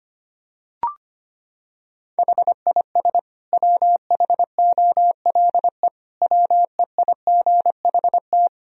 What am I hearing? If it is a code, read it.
Morse code, heard as a keyed tone, about 25 wpm, 700 Hz. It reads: HIS WHOLE WEIGHT
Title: Sentences from Top 1000 Words - Review Part 3 - 25wpm